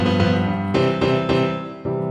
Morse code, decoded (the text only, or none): none